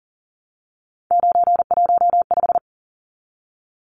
Morse code, heard as a keyed tone, about 38 wpm, 700 Hz. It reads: 915
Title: Numbers - 3-Digit 40wpm